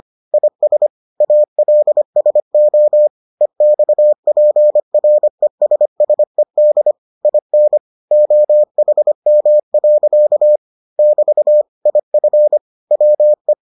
Morse code, heard as a keyed tone, about 25 wpm, 600 Hz. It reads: IS ALSO EXPRESSED IN OHM. <BT> IF WE